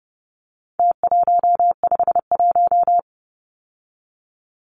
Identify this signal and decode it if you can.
Morse code, heard as a keyed tone, about 30 wpm, 700 Hz. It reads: T151
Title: Numbers - Contest Style - 4-Digit 30wpm